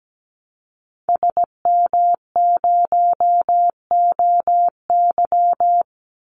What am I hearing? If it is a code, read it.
Morse code, heard as a keyed tone, about 17 wpm, 700 Hz. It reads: SM0OY
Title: Call Signs - 17wpm